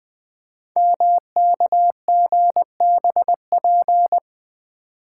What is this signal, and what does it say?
Morse code, heard as a keyed tone, about 20 wpm, 700 Hz. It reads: MKGBP
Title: Code Groups - 5 Letters - 20wpm